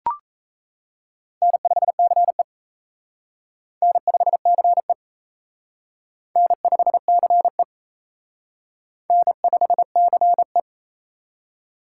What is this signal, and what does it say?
Morse code, heard as a keyed tone, about 36 wpm, 700 Hz. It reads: N5CE N5CE N5CE N5CE